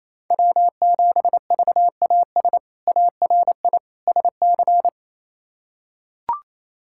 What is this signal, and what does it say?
Morse code, heard as a keyed tone, about 28 wpm, 700 Hz. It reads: W7VAS ARI SC
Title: K1USN SST QSO Practice - 28wpm